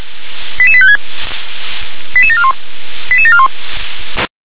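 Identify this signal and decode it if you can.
Five-tone selective call (ZVEI), heard as quick runs of five tones, tones of about 70 ms. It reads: 88856 88741 88741
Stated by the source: ZVEI five-tone call received with an SDR.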